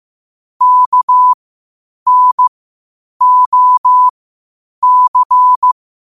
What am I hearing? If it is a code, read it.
Morse code, heard as a keyed tone, about 15 wpm, spaced as for 8 wpm, 1000 Hz. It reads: KNOC
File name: qotdfiles / 8wpm